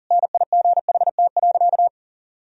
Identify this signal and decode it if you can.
Morse code, heard as a keyed tone, about 40 wpm, 700 Hz. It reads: NIGHT.
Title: Sentences from Top 500 Words and Abbreviations 40wpm